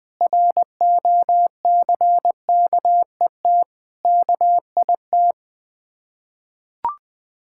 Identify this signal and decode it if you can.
Morse code, heard as a keyed tone, about 20 wpm, 700 Hz. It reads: ROCKET KIT
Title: Difficult Words with Hints - 20wpm